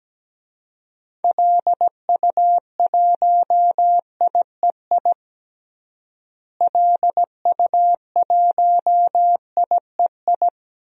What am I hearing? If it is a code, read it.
Morse code, heard as a keyed tone, about 17 wpm, 700 Hz. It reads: LU1IEI LU1IEI
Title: DXCC-100 Call Signs - 6 Characters - Repeated 2x - 17wpm